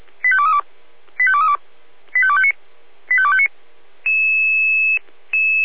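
Five-tone selective call (ZVEI), heard as quick runs of five tones, tones of about 70 ms. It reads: 85232 85232 85259 85259
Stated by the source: ZVEI five-tone call 85250 received by SDR at Feuerwehr Orlamünde (f85360).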